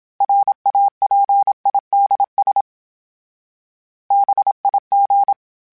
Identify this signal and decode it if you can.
Morse code, heard as a keyed tone, about 27 wpm, 800 Hz. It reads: RAPIDS BIG
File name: fromSMs / FromSMs mp3s / Cities@25